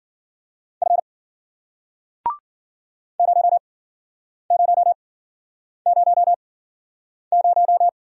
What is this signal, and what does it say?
Morse code, heard as a keyed tone, about 52 wpm, spaced as for 9 wpm, 700 Hz. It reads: F 0000